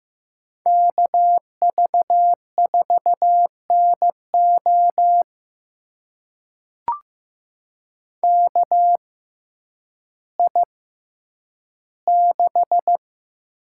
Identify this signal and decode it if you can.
Morse code, heard as a keyed tone, about 15 wpm, spaced as for 9 wpm, 700 Hz. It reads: KV4NO KI6
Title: USA Call Signs - 5 Characters - 6x Character Spacing - 15wpm